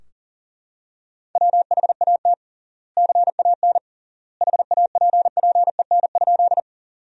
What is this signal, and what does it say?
Morse code, heard as a keyed tone, about 40 wpm, 700 Hz. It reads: WHAT CAN HAPPEN?
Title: 3x Word-Spacing - Sentences from Top 400 Words - 40wpm